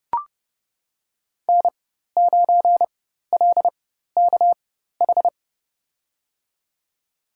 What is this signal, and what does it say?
Morse code, heard as a keyed tone, about 30 wpm, spaced as for 14 wpm, 700 Hz. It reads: N9LKH
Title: USA Call Signs - 5 Characters - 4x Character Spacing - 30wpm